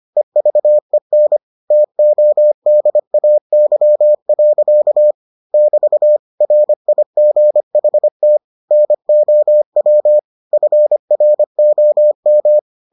Morse code, heard as a keyed tone, about 25 wpm, 600 Hz. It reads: EVEN TODAY. <BT> RIGHT NOW FROM